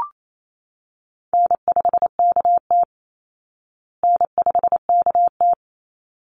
Morse code, heard as a keyed tone, about 28 wpm, 700 Hz. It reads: N5KT N5KT
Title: Easy Call Signs Repeated 2x - 28wpm